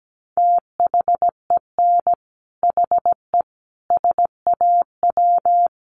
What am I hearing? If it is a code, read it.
Morse code, heard as a keyed tone, about 17 wpm, 700 Hz. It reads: THEN HE SAW
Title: Sentences from Top 400 Words 17wpm